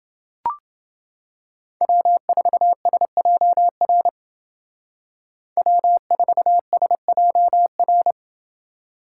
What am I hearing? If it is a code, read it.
Morse code, heard as a keyed tone, about 29 wpm, 700 Hz. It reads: W4SJR W4SJR